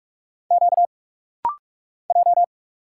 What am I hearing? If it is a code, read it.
Morse code, heard as a keyed tone, about 46 wpm, 700 Hz. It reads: Q J
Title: Single Letter-Number - Rapid-Fire 45wpm